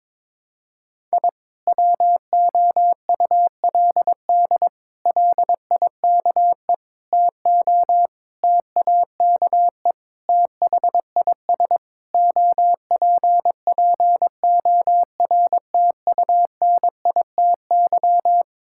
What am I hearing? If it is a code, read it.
Morse code, heard as a keyed tone, about 22 wpm, 700 Hz. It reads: I WOULD LIKE TO TAKE THIS OPPORTUNITY
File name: practice / 22wpm / Sets of 8 Words - 22wpm